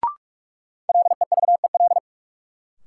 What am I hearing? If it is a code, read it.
Morse code, heard as a keyed tone, about 42 wpm, 700 Hz. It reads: LEVEL
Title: Sentences from Top 700 Words - New Word Drill - 45wpm